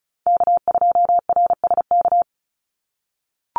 Morse code, heard as a keyed tone, about 36 wpm, 700 Hz. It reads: K2RSK
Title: Call Signs - 35wpm